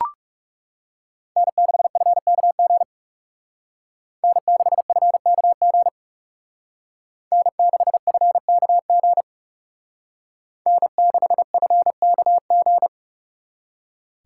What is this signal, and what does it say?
Morse code, heard as a keyed tone, about 37 wpm, 700 Hz. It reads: N6FKG N6FKG N6FKG N6FKG
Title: Speed Racing - USA 1x3 Call Signs 30wpm